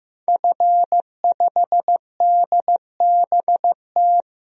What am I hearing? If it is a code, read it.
Morse code, heard as a keyed tone, about 15 wpm, 700 Hz. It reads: F5DBT
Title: DXCC-100 Call Signs - 5 Characters - 15wpm